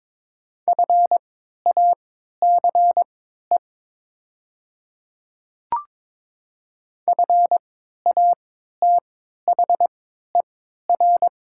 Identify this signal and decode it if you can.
Morse code, heard as a keyed tone, about 22 wpm, spaced as for 12 wpm, 700 Hz. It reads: FACE FATHER
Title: 3x Character Spacing - Top 300 Words - 22wpm